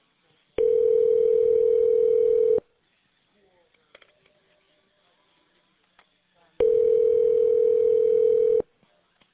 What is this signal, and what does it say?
Ringback tone (North America), 440 + 480 Hz.